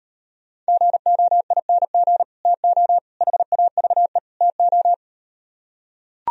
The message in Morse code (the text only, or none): GOING TO HAVE TO